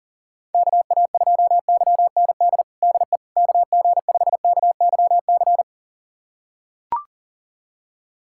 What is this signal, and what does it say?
Morse code, heard as a keyed tone, about 40 wpm, 700 Hz. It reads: KA2YND DE KG5KYC
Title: USA Callsign DE Callsign - 6 Characters - 40wpm